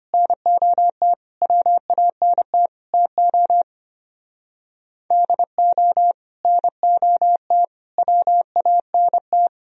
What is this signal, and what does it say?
Morse code, heard as a keyed tone, about 31 wpm, 700 Hz. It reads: NOT WANT TO DO NOT WANT